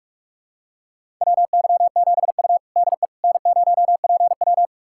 Morse code, heard as a keyed tone, about 45 wpm, 700 Hz. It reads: WY7U DE N0PW
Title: USA Callsign DE Callsign - 4 Characters - 45wpm